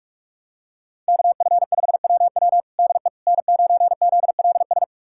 Morse code, heard as a keyed tone, about 46 wpm, 700 Hz. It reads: KF5WW DE N9ZLS
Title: USA Callsign DE Callsign - 5 Characters - 45wpm